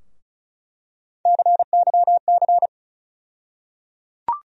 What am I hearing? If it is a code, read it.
Morse code, heard as a keyed tone, about 34 wpm, 700 Hz. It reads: CYC
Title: USA Call Sign Postfixes - 3 Letters 35wpm